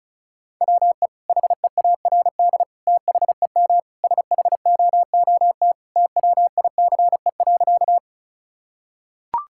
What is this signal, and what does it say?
Morse code, heard as a keyed tone, about 36 wpm, 700 Hz. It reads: WE HEARD THEM SHOOT TWICE.
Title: Sentences from Top 1600 Words 35wpm